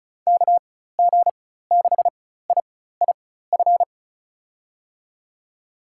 Morse code, heard as a keyed tone, about 34 wpm, spaced as for 16 wpm, 700 Hz. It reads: KG6IIF
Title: USA Call Signs - 6 Characters - 4x Character Spacing - 35wpm